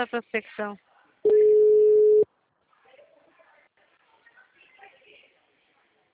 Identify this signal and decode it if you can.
Ringback tone (Europe, 425 Hz), with speech.
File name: out- -2007- - - .81604